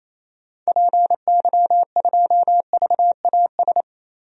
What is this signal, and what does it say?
Morse code, heard as a keyed tone, about 28 wpm, 700 Hz. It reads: PY2VAS
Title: DXCC-100 Call Signs - 6 Characters - Repeated 2x - 28wpm